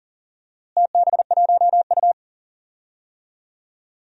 Morse code, heard as a keyed tone, about 40 wpm, 700 Hz. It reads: TB1U
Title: DXCC-100 Call Signs - 4 Characters - Repeated 2x - 40wpm